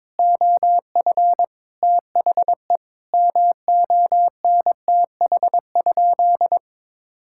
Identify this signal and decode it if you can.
Morse code, heard as a keyed tone, about 22 wpm, 700 Hz. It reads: OF THE MONTH?
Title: Sentences from Top 1000 Words - Review Part 6 - 22wpm